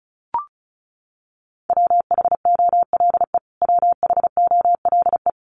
Morse code, heard as a keyed tone, about 34 wpm, 700 Hz. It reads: WHOLE WHOLE
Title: Top 500 Words Repeated 35wpm